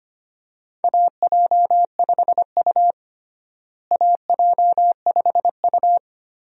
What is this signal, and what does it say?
Morse code, heard as a keyed tone, about 25 wpm, 700 Hz. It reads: AJ5U AJ5U
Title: USA Call Signs - 4 Characters - Repeated 3x - 25wpm